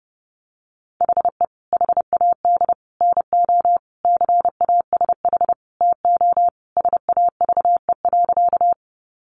Morse code, heard as a keyed tone, about 30 wpm, 700 Hz. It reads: HE HAD NO CASH TO SAVE.
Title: Sentences from Top 1100 Words 30wpm